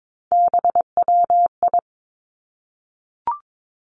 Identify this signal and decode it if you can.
Morse code, heard as a keyed tone, about 22 wpm, 700 Hz. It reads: BWI
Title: Three Challenging Letter-Number Combinations 22wpm